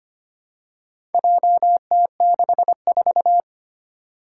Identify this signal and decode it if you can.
Morse code, heard as a keyed tone, about 25 wpm, 700 Hz. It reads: JT64
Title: Four Challenging Letter-Number Combinations 25wpm